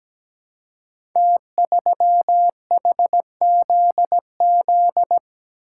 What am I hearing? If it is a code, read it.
Morse code, heard as a keyed tone, about 17 wpm, 700 Hz. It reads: T3HZZ
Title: Code Groups - 5 Letter:Numbers - 17wpm